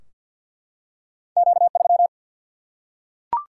Morse code, heard as a keyed tone, about 52 wpm, 700 Hz. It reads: Q3
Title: Single-Letter Single-Number 50wpm